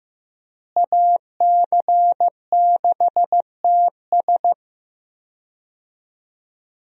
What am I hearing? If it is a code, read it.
Morse code, heard as a keyed tone, about 15 wpm, 700 Hz. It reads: AC6TS
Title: USA Call Signs - 5 Characters - Repeated 2x - 15wpm